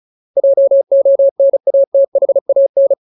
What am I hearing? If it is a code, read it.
Morse code, heard as a keyed tone, about 35 wpm, 550 Hz. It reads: JONATHAN
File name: Ultimate Word List / 35WPM Ultimate